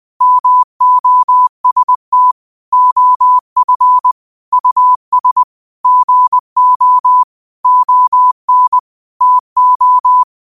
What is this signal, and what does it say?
Morse code, heard as a keyed tone, about 20 wpm, 1000 Hz. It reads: MOST OF US GO ON TO